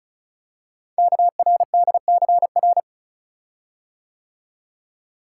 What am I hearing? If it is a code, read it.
Morse code, heard as a keyed tone, about 36 wpm, 690 Hz. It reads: KRDCR
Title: Code Groups - 5 Letters - 35wpm